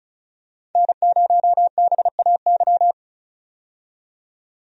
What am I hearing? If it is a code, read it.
Morse code, heard as a keyed tone, about 36 wpm, 700 Hz. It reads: N0BAY
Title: USA Call Signs - 5 Characters - 35wpm